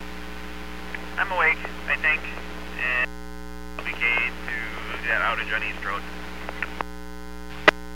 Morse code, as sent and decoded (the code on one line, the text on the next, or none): none